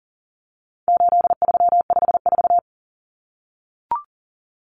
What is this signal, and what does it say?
Morse code, heard as a keyed tone, about 40 wpm, 700 Hz. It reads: 8354
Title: Numbers - 4-Digit 40wpm